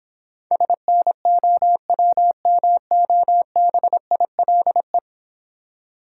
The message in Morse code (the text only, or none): SNOWMOBILE